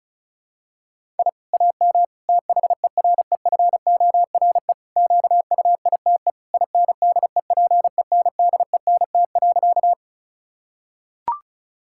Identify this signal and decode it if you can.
Morse code, heard as a keyed tone, about 36 wpm, 700 Hz. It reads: I AM THEREFORE QUITE INDEPENDENT.
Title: Sentences from Top 1000 Words - Review Part 6 - 35wpm